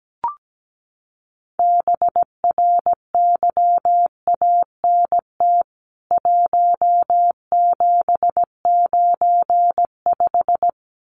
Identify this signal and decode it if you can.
Morse code, heard as a keyed tone, about 17 wpm, 700 Hz. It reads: BRYANT 1795
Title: CWT Exchange 17wpm